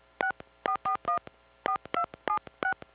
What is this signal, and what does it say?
DTMF, heard as telephone touch tones, tones of about 90 ms. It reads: 64414276